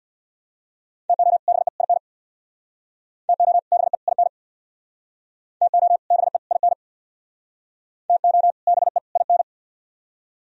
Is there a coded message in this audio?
Yes: Morse code from 1.1 to 9.4 s, heard as a keyed tone, about 66 wpm, 700 Hz. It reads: TO BE IN TO BE IN TO BE IN TO BE IN